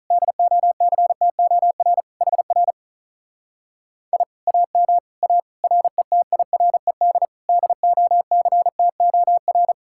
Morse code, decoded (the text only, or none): DOCTOR HR I AM A RETIRED DOCTOR